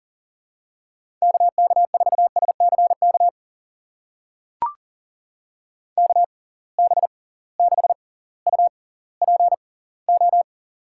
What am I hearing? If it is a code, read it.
Morse code, heard as a keyed tone, about 38 wpm, spaced as for 20 wpm, 700 Hz. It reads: KK4SCK KB6UPO
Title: USA Call Signs - 6 Characters - 6x Character Spacing - 40wpm